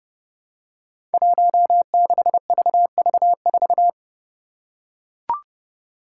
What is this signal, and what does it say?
Morse code, heard as a keyed tone, about 30 wpm, 700 Hz. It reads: 16VV4
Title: Five Challenging Letter-Number Combinations 30wpm